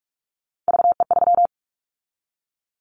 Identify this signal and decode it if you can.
Morse code, heard as a keyed tone, about 44 wpm, 700 Hz. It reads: VE3
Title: Callsign Prefixes for Top 100 DXCC - 45wpm